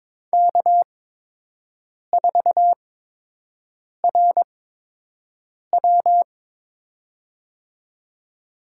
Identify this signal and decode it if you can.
Morse code, heard as a keyed tone, about 22 wpm, spaced as for 6 wpm, 700 Hz. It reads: K4RW
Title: USA Call Signs - 4 Characters - 8x Character Spacing - 22wpm